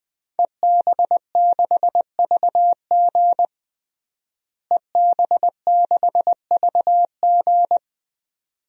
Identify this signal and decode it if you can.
Morse code, heard as a keyed tone, about 20 wpm, 700 Hz. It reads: EB6VG EB6VG
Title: DXCC-100 Call Signs - 5 Characters - Repeated 2x - 20wpm